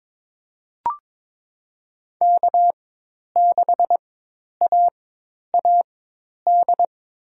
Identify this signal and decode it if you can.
Morse code, heard as a keyed tone, about 22 wpm, spaced as for 10 wpm, 700 Hz. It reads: K6AAD
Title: USA Call Signs - 5 Characters - 4x Character Spacing - 22wpm